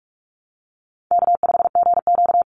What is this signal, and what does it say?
Morse code, heard as a keyed tone, about 44 wpm, 700 Hz. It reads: K5GQ
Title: Easy Call Signs - 45wpm